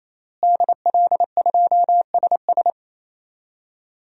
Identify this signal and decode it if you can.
Morse code, heard as a keyed tone, about 28 wpm, 700 Hz. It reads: DL2SS
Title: DXCC-100 Call Signs - 5 Characters - Repeated 3x - 28wpm